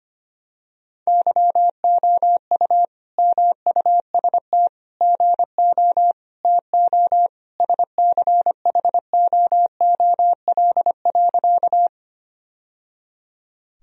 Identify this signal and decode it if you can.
Morse code, heard as a keyed tone, about 25 wpm, 700 Hz. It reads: YOU MUST GO TO SCHOOL.
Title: Sentences from Top 1000 Words - Review Part 1 - 25wpm